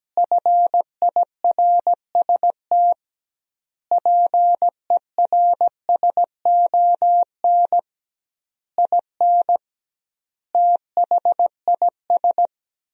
Morse code, heard as a keyed tone, about 17 wpm, 700 Hz. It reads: FIRST PERSON IN THIS
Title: 2x Word-Spacing - Sentences from Top 400 Words - 17wpm